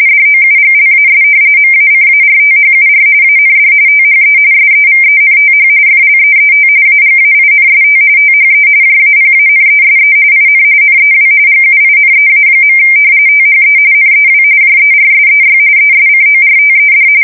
RTTY, about 45.45 baud. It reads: DOUGLAS MACARTHUR AND THE SEVENTH FLEET AGAINST THE JAPANESE IN NEW GUINEA. THE NEW JERSEY ESCORTED
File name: fsk170 / battleshipnj